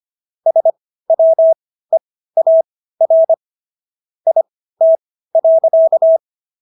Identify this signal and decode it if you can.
Morse code, heard as a keyed tone, about 25 wpm, spaced as for 15 wpm, 650 Hz. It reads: SWEAR IT.